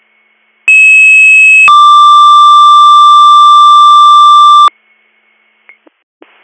Two-tone page, heard as a one-second tone, then a three-second tone, about 2600 then 1180 Hz. It is heard from 0.7 to 4.7 s.